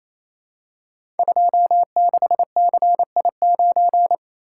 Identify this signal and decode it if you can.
Morse code, heard as a keyed tone, about 29 wpm, 700 Hz. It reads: 26CI9
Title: Code Groups - 5 Letter:Numbers - 28wpm